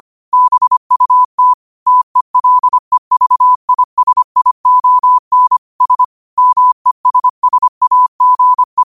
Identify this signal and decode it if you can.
Morse code, heard as a keyed tone, about 25 wpm, 1000 Hz. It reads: BUT TELEVISION S MESSAGE
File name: 25wpm